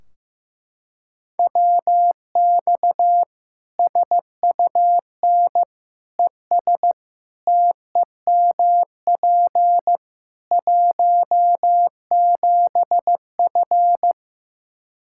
Morse code, heard as a keyed tone, about 16 wpm, 700 Hz. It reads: WX SUN ES TEMP 17F
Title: Practice QSO - Exchange 2 - Info - 15wpm